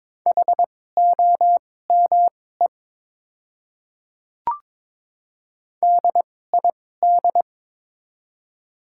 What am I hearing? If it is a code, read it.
Morse code, heard as a keyed tone, about 22 wpm, spaced as for 16 wpm, 690 Hz. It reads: HOME DID
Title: 2x Character Spacing - Top 500 Words - 22wpm